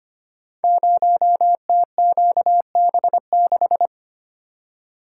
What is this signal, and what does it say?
Morse code, heard as a keyed tone, about 25 wpm, 700 Hz. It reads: 0TQB6
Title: Code Groups - 5 Letter:Numbers - 25wpm